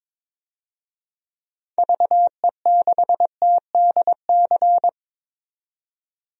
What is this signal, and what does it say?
Morse code, heard as a keyed tone, about 22 wpm, 700 Hz. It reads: VE6TDC
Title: DXCC-100 Call Signs - 6 Characters - 22wpm